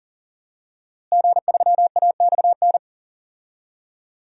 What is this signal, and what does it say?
Morse code, heard as a keyed tone, about 40 wpm, 700 Hz. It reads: G3AXN
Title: DXCC-100 Call Signs - 5 Characters - Repeated 3x - 40wpm